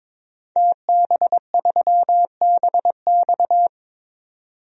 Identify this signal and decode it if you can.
Morse code, heard as a keyed tone, about 22 wpm, 700 Hz. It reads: TB3BX
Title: DXCC-100 Call Signs - 5 Characters - 22wpm